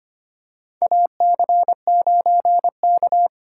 Morse code, heard as a keyed tone, about 25 wpm, 700 Hz. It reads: AC9K
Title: USA Call Signs - 4x Character Spacing - 25wpm